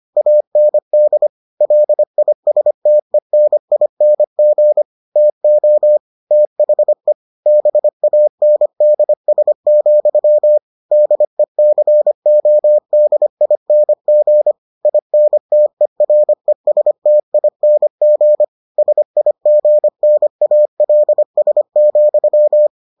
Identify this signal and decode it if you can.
Morse code, heard as a keyed tone, about 25 wpm, 600 Hz. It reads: AND LISTENING TO THE BANDS, DECODING INTERESTING SIGNALS,